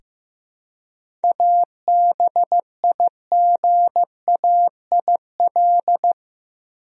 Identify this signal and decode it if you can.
Morse code, heard as a keyed tone, about 15 wpm, 700 Hz. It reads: ABIGAIL
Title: USA Common Names - 15wpm